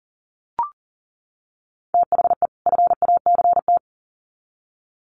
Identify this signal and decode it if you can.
Morse code, heard as a keyed tone, about 40 wpm, 700 Hz. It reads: THE FACT
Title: Sets of 2 Words 40wpm